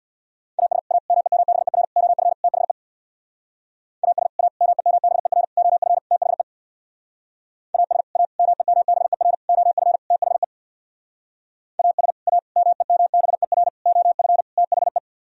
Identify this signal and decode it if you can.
Morse code, heard as a keyed tone, about 63 wpm, 700 Hz. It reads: AS A MEMBER OF THE AS A MEMBER OF THE AS A MEMBER OF THE AS A MEMBER OF THE